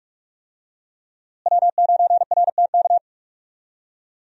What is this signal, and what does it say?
Morse code, heard as a keyed tone, about 44 wpm, 700 Hz. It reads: W9RTK
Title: USA Call Signs - 5 Characters - Repeated 2x - 45wpm